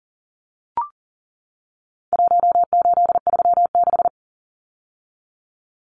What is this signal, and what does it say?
Morse code, heard as a keyed tone, about 40 wpm, 700 Hz. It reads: 1836
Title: Numbers - Contest Style - 4-Digit 40wpm